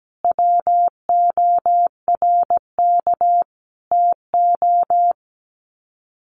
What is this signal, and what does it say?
Morse code, heard as a keyed tone, about 17 wpm, 700 Hz. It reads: WORK TO
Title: Sets of 5 Words Encore2 17wpm